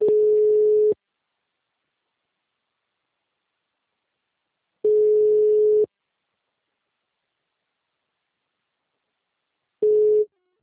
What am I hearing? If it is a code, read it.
Ringback tone (Europe, 425 Hz).